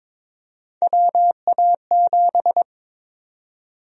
Morse code, heard as a keyed tone, about 22 wpm, 700 Hz. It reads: WA7
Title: USA Call Sign Prefixes - 22wpm